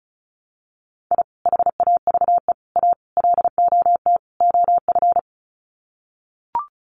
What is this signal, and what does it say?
Morse code, heard as a keyed tone, about 34 wpm, 700 Hz. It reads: I HAVE A LOT OF
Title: Sets of 5 Words 35wpm